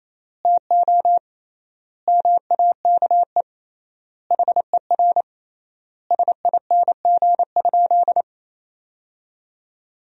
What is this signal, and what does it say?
Morse code, heard as a keyed tone, about 28 wpm, 700 Hz. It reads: TO MAKE HER SING?
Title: 3x Word-Spacing - Sentences from Top 400 Words - 28wpm